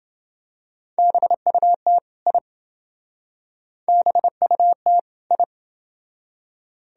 Morse code, heard as a keyed tone, about 28 wpm, 700 Hz. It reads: BUT I BUT I